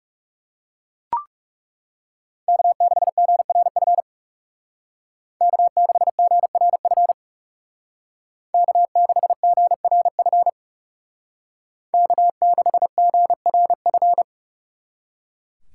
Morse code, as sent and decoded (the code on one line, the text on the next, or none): -.- -.... --. .-. ..-. / -.- -.... --. .-. ..-. / -.- -.... --. .-. ..-. / -.- -.... --. .-. ..-.
K6GRF K6GRF K6GRF K6GRF